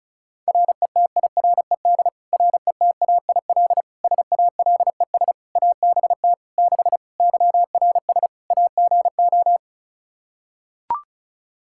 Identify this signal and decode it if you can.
Morse code, heard as a keyed tone, about 36 wpm, 700 Hz. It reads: RETIRED RETAIL SALES ABT 6 YRS AGO
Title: Practice QSO - Exchange 3 - More Info - 35wpm